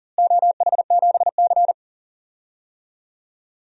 Morse code, heard as a keyed tone, about 40 wpm, 700 Hz. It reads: OH7C
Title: DXCC-100 Call Signs - 4 Characters - Repeated 2x - 40wpm